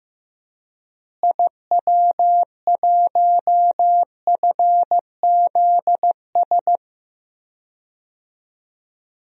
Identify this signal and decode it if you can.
Morse code, heard as a keyed tone, about 15 wpm, 700 Hz. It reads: IW1FZS
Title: DXCC-100 Call Signs - 6 Characters - Repeated 2x - 15wpm